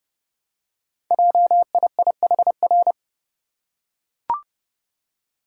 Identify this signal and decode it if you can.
Morse code, heard as a keyed tone, about 28 wpm, 700 Hz. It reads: JIIHR
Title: Code Groups - 5 Letters - 30wpm